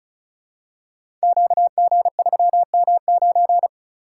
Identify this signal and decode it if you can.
Morse code, heard as a keyed tone, about 36 wpm, 700 Hz. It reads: QG3M9
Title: Code Groups - 5 Letter:Numbers - 35wpm